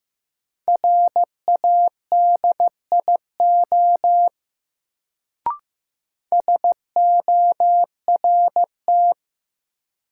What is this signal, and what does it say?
Morse code, heard as a keyed tone, about 15 wpm, 700 Hz. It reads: RADIO SORT
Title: Sentences from Top 1000 Words - Review Part 3 - New Word Drill - 15wpm